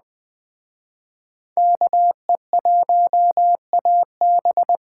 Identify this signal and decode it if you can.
Morse code, heard as a keyed tone, about 21 wpm, 700 Hz. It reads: KE1AB
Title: Speed Racing - USA 2x2 Call Signs 17wpm